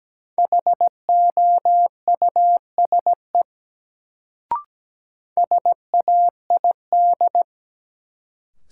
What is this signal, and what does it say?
Morse code, heard as a keyed tone, about 17 wpm, 700 Hz. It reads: HOUSE SAID
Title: Sets of 2 Words - New Word Drill - 17wpm